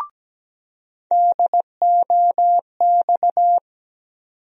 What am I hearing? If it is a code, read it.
Morse code, heard as a keyed tone, about 17 wpm, 700 Hz. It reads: DOX
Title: USA Call Sign Postfixes - 3 Letters 17wpm